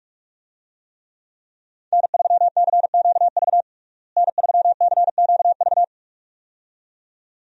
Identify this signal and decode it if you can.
Morse code, heard as a keyed tone, about 42 wpm, 700 Hz. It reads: N3CQV N3CQV
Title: USA Call Signs - 5 Characters - Repeated 2x - 45wpm